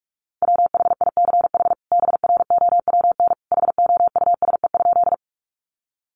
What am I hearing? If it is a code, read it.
Morse code, heard as a keyed tone, about 44 wpm, 700 Hz. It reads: WHICH BROWN HOUSE?
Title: Sentences from Top 1200 Words 45wpm